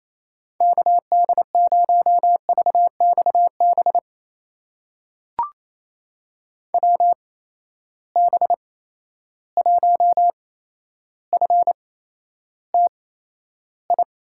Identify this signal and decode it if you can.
Morse code, heard as a keyed tone, about 27 wpm, spaced as for 12 wpm, 700 Hz. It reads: KD0VXB WB1FTI
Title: USA Call Signs - 6 Characters - 8x Character Spacing - 28wpm